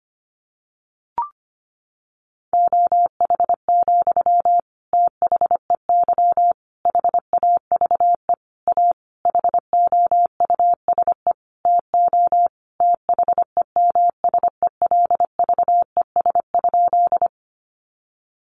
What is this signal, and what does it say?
Morse code, heard as a keyed tone, about 25 wpm, 700 Hz. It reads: OH, THEY HAVE A HOUSE TO THEMSELVES?
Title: Sentences from Top 800 Words 25wpm